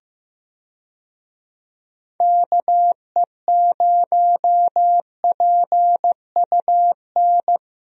Morse code, heard as a keyed tone, about 16 wpm, 700 Hz. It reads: KE0PUN
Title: USA Call Signs - 6 Characters - Repeated 2x - 15wpm